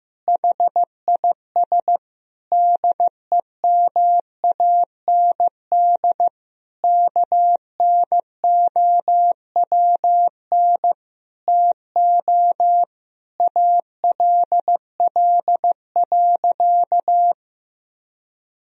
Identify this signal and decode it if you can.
Morse code, heard as a keyed tone, about 15 wpm, 700 Hz. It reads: HIS DEMAND KNOWN TO ALL.
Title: Sentences from Top 1000 Words - Review Part 4 - 15wpm